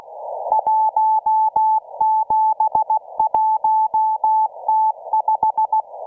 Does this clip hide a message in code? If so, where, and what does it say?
Yes: Morse code from 0.5 to 5.8 s, heard as a keyed tone, about 16 wpm, 870 Hz. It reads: 171T5